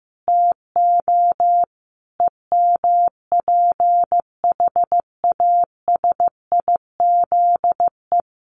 Morse code, heard as a keyed tone, about 14 wpm, 700 Hz. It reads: TO EMPHASIZE
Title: Sentences from Top 1500 Words 15wpm